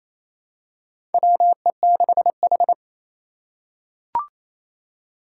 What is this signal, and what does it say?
Morse code, heard as a keyed tone, about 28 wpm, 700 Hz. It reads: WE6H
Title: USA Call Signs - 4 Characters - 3x Character Spacing - 28wpm